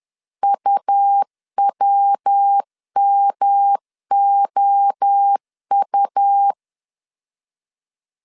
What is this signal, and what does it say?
Morse code, heard as a keyed tone, about 10 wpm, 800 Hz. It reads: UWMOU
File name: CW / DET / 1 9 1500C S B 10wpm 4439